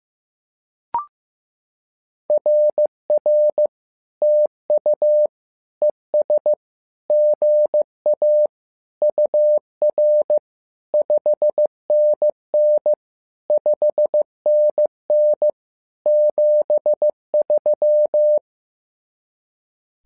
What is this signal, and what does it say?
Morse code, heard as a keyed tone, about 15 wpm, 600 Hz. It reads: RR TU ES GA UR 5NN 5NN 73
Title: SOTA QSO Practice 15wpm